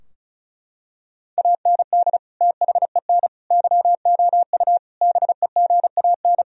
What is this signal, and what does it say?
Morse code, heard as a keyed tone, about 36 wpm, 700 Hz. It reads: AND THEN YOU BEGAN
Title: Sentences from Top 1000 Words - Review Part 2 - 35wpm